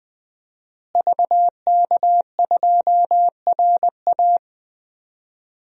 Morse code, heard as a keyed tone, about 21 wpm, 700 Hz. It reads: VK2RA
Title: DXCC-100 Call Signs - 5 Characters - Repeated 3x - 20wpm